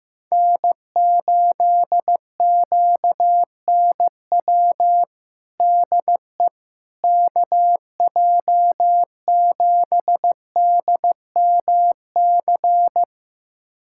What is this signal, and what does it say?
Morse code, heard as a keyed tone, about 15 wpm, 700 Hz. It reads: N8QNW DE KJ7DMC
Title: USA Callsign DE Callsign 15wpm